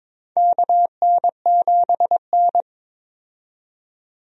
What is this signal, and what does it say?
Morse code, heard as a keyed tone, about 21 wpm, 700 Hz. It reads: KN7N
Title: USA Call Signs - 4 Characters - 22wpm